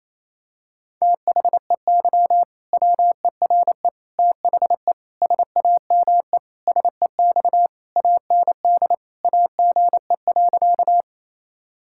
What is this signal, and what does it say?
Morse code, heard as a keyed tone, about 28 wpm, 700 Hz. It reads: THEY WERE THE SAME SEX AND AGE.
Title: Sentences from Top 1600 Words 28wpm